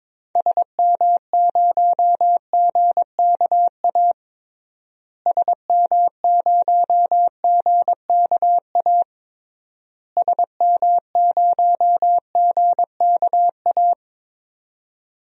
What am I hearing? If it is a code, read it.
Morse code, heard as a keyed tone, about 22 wpm, 700 Hz. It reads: SM0GKA SM0GKA SM0GKA
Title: DXCC-100 Call Signs - 6 Characters - Repeated 3x - 22wpm